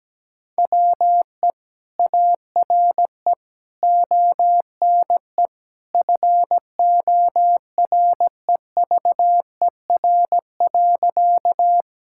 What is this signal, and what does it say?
Morse code, heard as a keyed tone, about 17 wpm, 700 Hz. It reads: WE ARE ONE FOREVER.
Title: Sentences from Top 1600 Words 17wpm